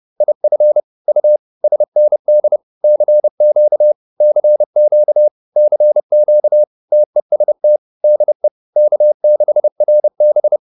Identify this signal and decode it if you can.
Morse code, heard as a keyed tone, about 30 wpm, 600 Hz. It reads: IF U SND CQ CQ CQ TEST DE K6RB